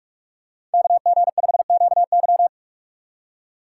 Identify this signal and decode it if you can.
Morse code, heard as a keyed tone, about 46 wpm, 700 Hz. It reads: KG5QY
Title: USA Call Signs - 6x Character Spacing - 45wpm